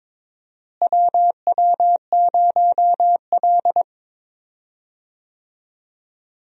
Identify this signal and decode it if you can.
Morse code, heard as a keyed tone, about 22 wpm, 700 Hz. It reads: WW0L
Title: USA Call Signs - 4 Characters - Repeated 2x - 22wpm